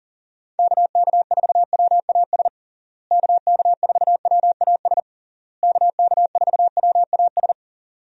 Morse code, heard as a keyed tone, about 38 wpm, 700 Hz. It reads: KK4WAS KK4WAS KK4WAS
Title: USA Call Signs - 6 Characters - Repeated 3x - 40wpm